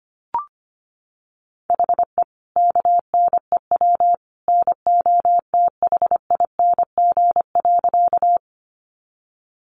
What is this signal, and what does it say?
Morse code, heard as a keyed tone, about 25 wpm, 700 Hz. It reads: HE KNEW NOTHING.
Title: Sentences from Top 500 Words 25wpm